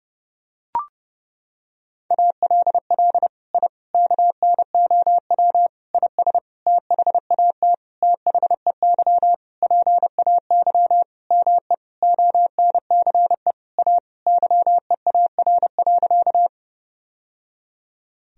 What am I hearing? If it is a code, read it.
Morse code, heard as a keyed tone, about 30 wpm, 700 Hz. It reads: ALL I KNOW IS THAT THEY PAY ME ONCE A YEAR.
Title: Sentences from Top 1000 Words - Review Part 1 - 30wpm